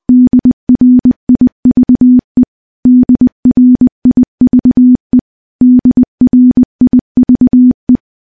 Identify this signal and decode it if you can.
Morse code, heard as a keyed tone, about 20 wpm, 250 Hz. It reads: DRIVE DRIVE DRIVE
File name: morse / morse-training-20WPM